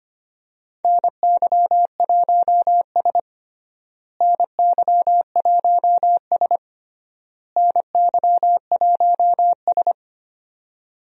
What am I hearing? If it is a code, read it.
Morse code, heard as a keyed tone, about 25 wpm, 700 Hz. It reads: NY1S NY1S NY1S
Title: USA Call Signs - 4 Characters - Repeated 3x - 25wpm